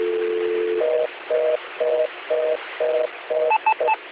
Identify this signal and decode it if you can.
Telephone call progress tone (North America), heard as a telephone network's tone, about 480, 620 Hz, with noise. Reorder tone.